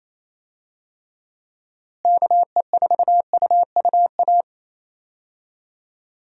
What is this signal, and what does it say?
Morse code, heard as a keyed tone, about 28 wpm, 700 Hz. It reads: KE4UUA
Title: USA Call Signs - 6 Characters - Repeated 2x - 28wpm